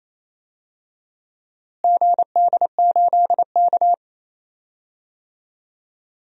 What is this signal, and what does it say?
Morse code, heard as a keyed tone, about 28 wpm, 700 Hz. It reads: GD8K
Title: DXCC-100 Call Signs - 4 Characters - Repeated 3x - 28wpm